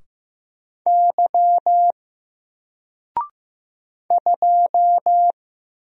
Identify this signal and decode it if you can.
Morse code, heard as a keyed tone, about 15 wpm, 700 Hz. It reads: Y 2
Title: Single Character (Letter:Number:Punctuation:Prosign) 15wpm